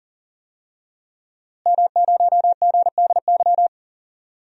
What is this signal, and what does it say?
Morse code, heard as a keyed tone, about 37 wpm, 700 Hz. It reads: M0GDY
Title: DXCC-100 Call Signs - 5 Characters - Repeated 3x - 40wpm